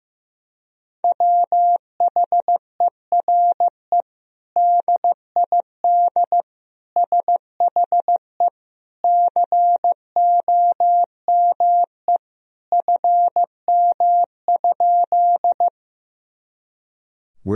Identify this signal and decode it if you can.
Morse code, heard as a keyed tone, about 15 wpm, 700 Hz. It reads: WHERE DID SHE COME FM?
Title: Sentences from Top 200 Words and Abbreviations 15wpm